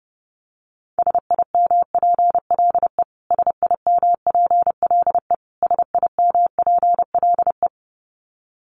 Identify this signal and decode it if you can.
Morse code, heard as a keyed tone, about 30 wpm, 700 Hz. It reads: SIMPLE SIMPLE SIMPLE
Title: Top 500 Words Repeated 30wpm